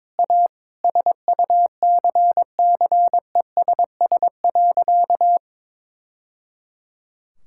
Morse code, heard as a keyed tone, about 22 wpm, 700 Hz. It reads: A SUCCESS.
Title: Sentences from Top 1000 Words - Review Part 5 - 22wpm